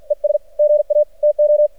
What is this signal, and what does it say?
Morse code, heard as a keyed tone, about 50 wpm, 600 Hz. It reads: ES GA TO
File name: archived / CRSnet